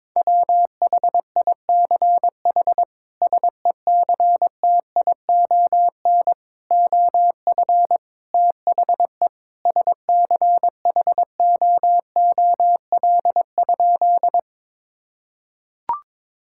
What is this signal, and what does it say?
Morse code, heard as a keyed tone, about 22 wpm, 700 Hz. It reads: WHICH SECTION OF THE SCHOOL?
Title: Sentences from Top 1000 Words - Review Part 5 - 22wpm